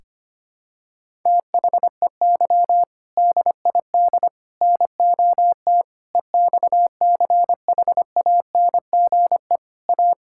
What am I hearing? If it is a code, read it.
Morse code, heard as a keyed tone, about 25 wpm, 700 Hz. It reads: THEY DID NOT EXCHANGE A
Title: Sentences from Top 1000 Words - Review Part 6 - 25wpm